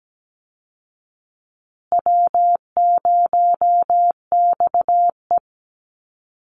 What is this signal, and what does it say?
Morse code, heard as a keyed tone, about 17 wpm, 700 Hz. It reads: W0XE
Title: Easy Call Signs Repeated 2x - 17wpm